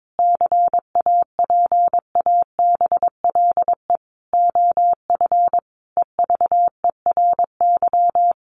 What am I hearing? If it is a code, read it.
Morse code, heard as a keyed tone, about 22 wpm, 700 Hz. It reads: CAPABLE OF EVERY
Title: Sentences from Top 1200 Words 22wpm